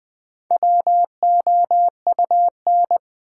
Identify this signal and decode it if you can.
Morse code, heard as a keyed tone, about 20 wpm, 700 Hz. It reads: WOUN
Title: Build A Word - 1 Letter At A Time - 5-Letter Words - 20wpm